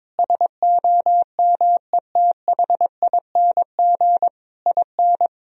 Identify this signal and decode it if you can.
Morse code, heard as a keyed tone, about 22 wpm, 700 Hz. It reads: SOMETHING IN
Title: Sentences from Top 1000 Words - Review Part 3 - 22wpm